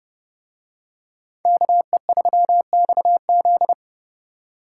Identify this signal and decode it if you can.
Morse code, heard as a keyed tone, about 31 wpm, 700 Hz. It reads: KE3XZ